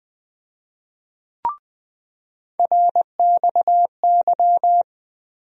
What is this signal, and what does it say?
Morse code, heard as a keyed tone, about 20 wpm, 700 Hz. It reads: RXY
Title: USA Call Sign Postfixes - 3 Letters 20wpm